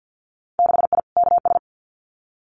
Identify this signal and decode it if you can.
Morse code, heard as a keyed tone, about 50 wpm, 700 Hz. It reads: 6I KS
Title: Winter Field Day Exchanges 50wpm